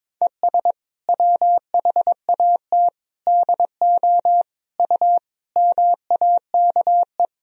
Morse code, heard as a keyed tone, about 21 wpm, 700 Hz. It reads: ES WHAT DO U MAKE
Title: Sentences from Top 100 Words and Abbreviations 22wpm